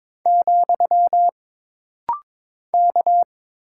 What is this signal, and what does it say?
Morse code, heard as a keyed tone, about 22 wpm, 700 Hz. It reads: , K
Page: This rapid-fire timing is geared to push your ICR (Instant Character Recognition) skills.